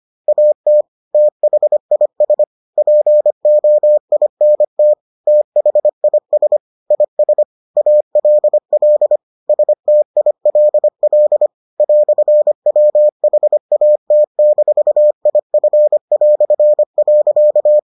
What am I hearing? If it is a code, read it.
Morse code, heard as a keyed tone, about 25 wpm, 600 Hz. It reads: AT THIS POINT THIS IS ALL STILL "WHAT-IF".